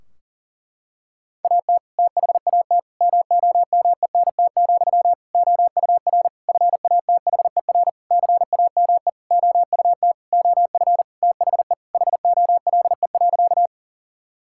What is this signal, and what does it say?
Morse code, heard as a keyed tone, about 37 wpm, 700 Hz. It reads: AT THAT MOMENT, OUR FATHER CAME OUT OF THE HOLE.
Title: Sentences from Top 1600 Words 40wpm